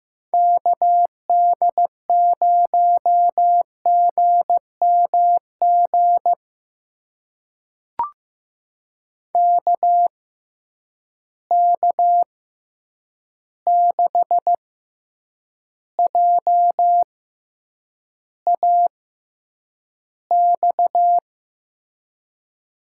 Morse code, heard as a keyed tone, about 15 wpm, spaced as for 8 wpm, 700 Hz. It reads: KD0GMG KK6JAX
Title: USA Call Signs - 6 Characters - 6x Character Spacing - 15wpm